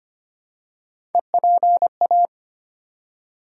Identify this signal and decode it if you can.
Morse code, heard as a keyed tone, about 25 wpm, 700 Hz. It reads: EPA
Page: morse-code-ninja-practice-25wpm